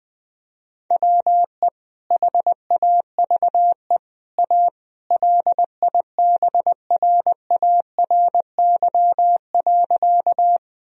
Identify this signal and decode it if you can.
Morse code, heard as a keyed tone, about 20 wpm, 700 Hz. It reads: WE HAVE A LIBRARY.
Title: Sentences from Top 1000 Words - Review Part 4 - 20wpm